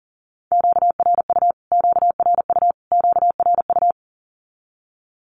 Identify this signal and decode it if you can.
Morse code, heard as a keyed tone, about 40 wpm, 700 Hz. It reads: QRU QRU QRU
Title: Q-Codes Repeated 3x 40wpm